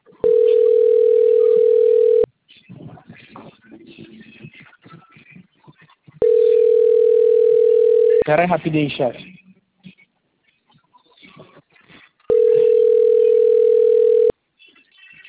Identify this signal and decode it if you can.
Ringback tone (North America), 440 + 480 Hz, with speech.
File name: out- - - - - .54061